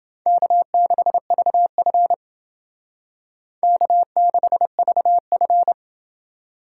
Morse code, heard as a keyed tone, about 29 wpm, 700 Hz. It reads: K6VF K6VF